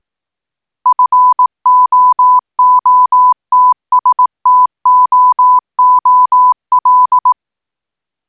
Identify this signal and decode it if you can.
Morse code, heard as a keyed tone, about 18 wpm, 1000 Hz. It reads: FOOTSTOOL
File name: footstool-18wpm-backwards-gsm